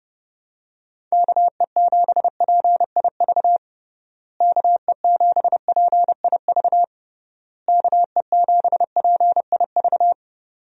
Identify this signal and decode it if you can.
Morse code, heard as a keyed tone, about 30 wpm, 700 Hz. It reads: KE7PIV KE7PIV KE7PIV
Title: USA Call Signs - 6 Characters - Repeated 3x - 30wpm